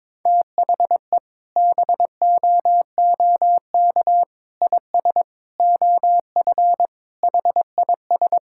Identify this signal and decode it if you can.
Morse code, heard as a keyed tone, about 22 wpm, 700 Hz. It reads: THE BOOK IS OF HIS
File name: practice / 22wpm / Sentences with 2-Word Phrases - 22wpm